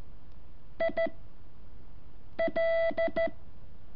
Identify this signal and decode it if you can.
Morse code, heard as a keyed tone, about 13 wpm, 660 Hz. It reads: I L
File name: curso / ejercicio 7